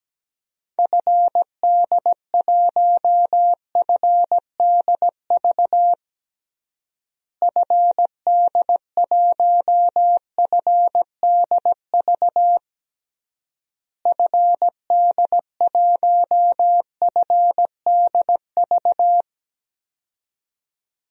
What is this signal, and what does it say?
Morse code, heard as a keyed tone, about 17 wpm, 700 Hz. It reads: FD1FDV FD1FDV FD1FDV
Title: DXCC-100 Call Signs - 6 Characters - Repeated 3x - 17wpm